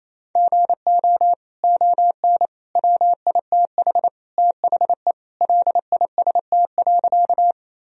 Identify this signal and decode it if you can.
Morse code, heard as a keyed tone, about 27 wpm, 700 Hz. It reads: GO ON WITH THE LIST.
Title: Sentences from Top 1000 Words - Review Part 2 - 28wpm